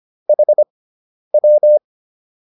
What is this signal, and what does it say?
Morse code, heard as a keyed tone, about 25 wpm, spaced as for 10 wpm, 600 Hz. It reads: HW
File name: QSO102 10